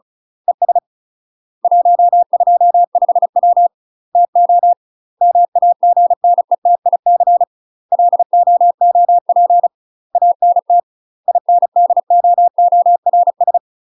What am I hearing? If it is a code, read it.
Morse code, heard as a keyed tone, about 34 wpm, 700 Hz. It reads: ES 125W TO MAGNETIC LOOP ANT INDOORS
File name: UK MASTER QSOs TRAINING / 35WPM / UK MASTER QSOs 0023